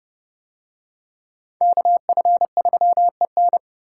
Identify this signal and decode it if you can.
Morse code, heard as a keyed tone, about 31 wpm, 700 Hz. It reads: KF3EN